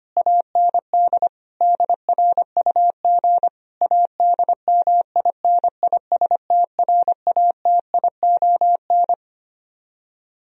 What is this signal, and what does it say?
Morse code, heard as a keyed tone, about 25 wpm, 700 Hz. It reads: AND DRUG ADMINISTRATION
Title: Speed-Racing - Sets of 5 Words 25wpm